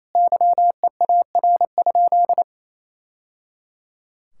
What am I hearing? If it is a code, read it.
Morse code, heard as a keyed tone, about 28 wpm, 700 Hz. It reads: YEAR?
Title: Sentences from Top 1000 Words - Review Part 6 - 28wpm